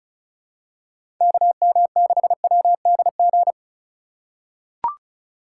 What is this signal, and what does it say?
Morse code, heard as a keyed tone, about 36 wpm, 700 Hz. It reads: KM6WDG
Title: USA Call Signs - 6 Characters - 6x Character Spacing - 35wpm